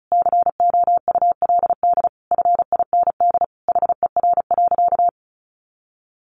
Morse code, heard as a keyed tone, about 34 wpm, 700 Hz. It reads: COULD FIND HER.
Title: Sentences from Top 1400 Words 35wpm